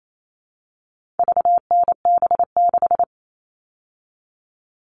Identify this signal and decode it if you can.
Morse code, heard as a keyed tone, about 28 wpm, 700 Hz. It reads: VNB6
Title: All Cut Numbers - 4-Digit 28wpm